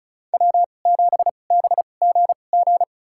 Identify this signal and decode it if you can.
Morse code, heard as a keyed tone, about 36 wpm, spaced as for 26 wpm, 700 Hz. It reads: W7BGG
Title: USA Call Signs - 5 Characters - 2x Character Spacing - 35wpm